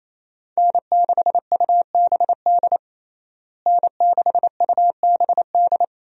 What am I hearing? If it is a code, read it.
Morse code, heard as a keyed tone, about 27 wpm, 700 Hz. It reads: N6UBD N6UBD
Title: USA Call Signs - 5 Characters - Repeated 3x - 28wpm